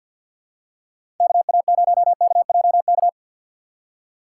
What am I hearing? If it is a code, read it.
Morse code, heard as a keyed tone, about 52 wpm, 700 Hz. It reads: KA0KJK
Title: USA Call Signs - 6 Characters - 3x Character Spacing - 50wpm